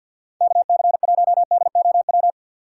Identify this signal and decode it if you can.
Morse code, heard as a keyed tone, about 52 wpm, 700 Hz. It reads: KC1DOW
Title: USA Call Signs - 2x Character Spacing - 50wpm